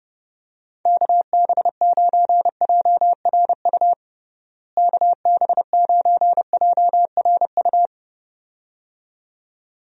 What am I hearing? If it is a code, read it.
Morse code, heard as a keyed tone, about 30 wpm, 700 Hz. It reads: KB9JRU KB9JRU
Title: USA Call Signs - 6 Characters - Repeated 2x - 30wpm